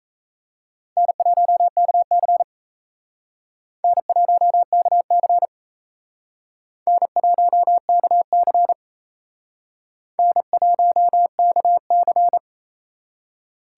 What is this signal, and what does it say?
Morse code, heard as a keyed tone, about 35 wpm, 700 Hz. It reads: N1KC N1KC N1KC N1KC
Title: Speed Racing - USA Call Signs 28wpm